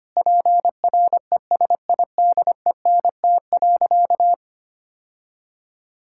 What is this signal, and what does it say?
Morse code, heard as a keyed tone, about 25 wpm, 700 Hz. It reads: PRESIDENT.
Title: Sentences from Top 1600 Words 25wpm